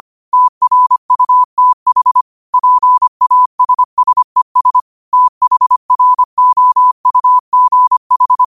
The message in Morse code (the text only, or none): TRUTH PASSES THROUGH